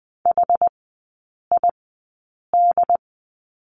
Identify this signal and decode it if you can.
Morse code, heard as a keyed tone, about 20 wpm, spaced as for 8 wpm, 700 Hz. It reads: HID
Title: 198 Lesson 32e - Letter k - Three Characters 20wpm